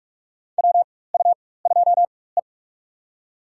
Morse code, heard as a keyed tone, about 43 wpm, spaced as for 21 wpm, 700 Hz. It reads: WU2E
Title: USA Call Signs - 4x Character Spacing - 45wpm